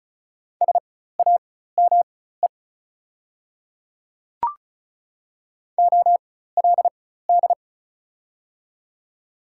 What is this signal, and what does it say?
Morse code, heard as a keyed tone, about 36 wpm, spaced as for 16 wpm, 690 Hz. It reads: SAME OLD
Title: 4x Character Spacing - Top 300 Words - 35wpm